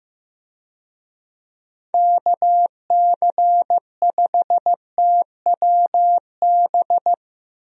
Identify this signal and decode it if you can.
Morse code, heard as a keyed tone, about 15 wpm, 700 Hz. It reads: KC5TWB
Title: USA Call Signs - 6 Characters - Repeated 2x - 15wpm